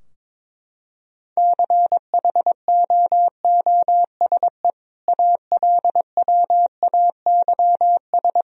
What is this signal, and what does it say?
Morse code, heard as a keyed tone, about 22 wpm, 700 Hz. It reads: CHOOSE ALWAYS
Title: Sentences from Top 1000 Words - Review Part 4 - 22wpm